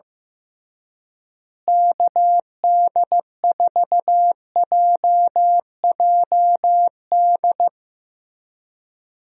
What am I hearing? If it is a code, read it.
Morse code, heard as a keyed tone, about 15 wpm, 700 Hz. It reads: KD4JJD
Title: USA Call Signs - 6 Characters - Repeated 2x - 15wpm